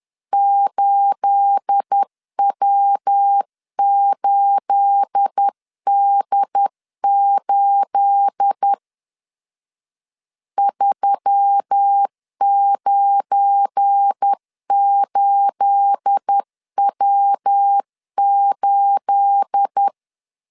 Morse code, heard as a keyed tone, about 10 wpm, 800 Hz. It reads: 8W8D8 398W8
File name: LEZIONI / 9 28 10wpm 2842